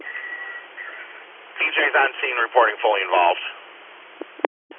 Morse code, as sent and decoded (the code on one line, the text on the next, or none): none